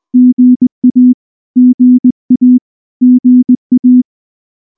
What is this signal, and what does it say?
Morse code, heard as a keyed tone, about 19 wpm, 250 Hz. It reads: GA GA GA
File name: morse / morse-training-20WPM-1